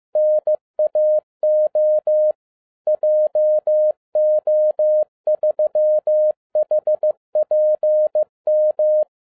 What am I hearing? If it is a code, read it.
Morse code, heard as a keyed tone, about 15 wpm, 610 Hz. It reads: NAO JO3HPM
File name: 2012-01-mp3 / n17-SUMMARY OF FEA NET IN 2011-15wpm